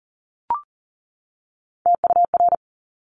Morse code, heard as a keyed tone, about 39 wpm, 700 Hz. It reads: TUR
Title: Top Three Letter Combinations in Words 40wpm